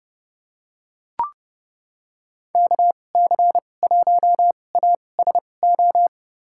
Morse code, heard as a keyed tone, about 32 wpm, spaced as for 22 wpm, 700 Hz. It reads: KC1ASO
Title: USA Call Signs - 2x Character Spacing - 30wpm